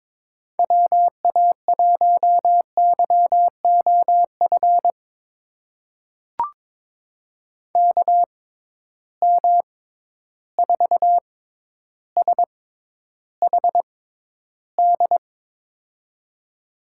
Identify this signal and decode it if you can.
Morse code, heard as a keyed tone, about 22 wpm, spaced as for 11 wpm, 700 Hz. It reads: WA1YOF KM4SHD
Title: USA Call Signs - 6 Characters - 6x Character Spacing - 22wpm